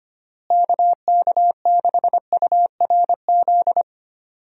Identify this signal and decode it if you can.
Morse code, heard as a keyed tone, about 25 wpm, 700 Hz. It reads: KK6URZ
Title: USA Call Signs - 6 Characters - Repeated 3x - 25wpm